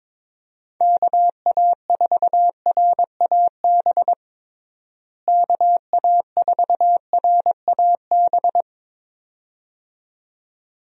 Morse code, heard as a keyed tone, about 21 wpm, 700 Hz. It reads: KA4RAB KA4RAB
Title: USA Call Signs - 6 Characters - Repeated 2x - 22wpm